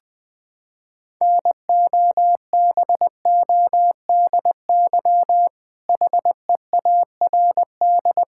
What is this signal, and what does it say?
Morse code, heard as a keyed tone, about 20 wpm, 700 Hz. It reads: NOBODY HEARD
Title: Sentences from Top 1600 Words 20wpm